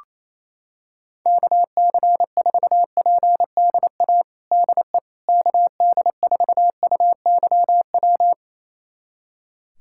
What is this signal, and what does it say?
Morse code, heard as a keyed tone, about 28 wpm, 700 Hz. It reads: KC4PDA DE KD4UYW
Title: USA Callsign DE Callsign - 6 Characters - 28wpm